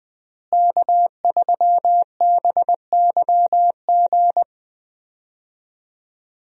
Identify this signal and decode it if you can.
Morse code, heard as a keyed tone, about 20 wpm, 700 Hz. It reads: K3BYG
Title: USA Call Signs - 5 Characters - Repeated 3x - 20wpm